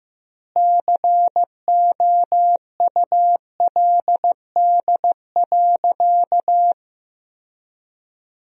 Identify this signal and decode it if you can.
Morse code, heard as a keyed tone, about 16 wpm, 700 Hz. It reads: COULD.
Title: 2x Word-Spacing - Sentences from Top 500 Words - 15wpm